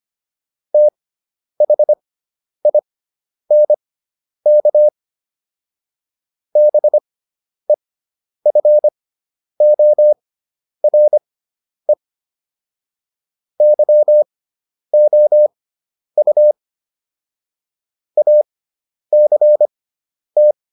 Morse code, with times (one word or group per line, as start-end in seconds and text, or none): 0.7-4.9 THINK
6.6-11.9 BEFORE
13.6-16.5 YOU
18.2-20.5 ACT